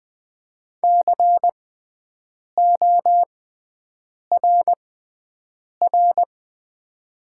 Morse code, heard as a keyed tone, about 20 wpm, spaced as for 7 wpm, 700 Hz. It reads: CORR
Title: 6x Character Spacing - Top 500 Words - 20wpm